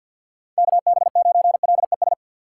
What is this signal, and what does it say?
Morse code, heard as a keyed tone, about 47 wpm, 690 Hz. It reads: KB9LES
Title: USA Call Signs - 3x Character Spacing - 50wpm